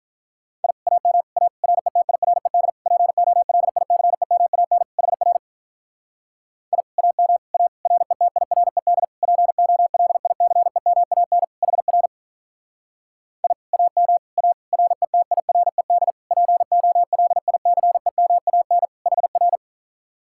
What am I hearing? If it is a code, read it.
Morse code, heard as a keyed tone, about 46 wpm, 700 Hz. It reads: I AM A RETIRED POLICEMAN HR I AM A RETIRED POLICEMAN HR I AM A RETIRED POLICEMAN HR